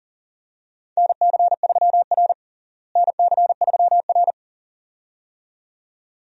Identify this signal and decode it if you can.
Morse code, heard as a keyed tone, about 40 wpm, 700 Hz. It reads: NC3R NC3R
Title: USA Call Signs - 4 Characters - Repeated 2x - 40wpm